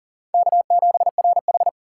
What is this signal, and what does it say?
Morse code, heard as a keyed tone, about 40 wpm, 700 Hz. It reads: K7RH
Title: USA Call Signs - 4 Characters - 3x Character Spacing - 40wpm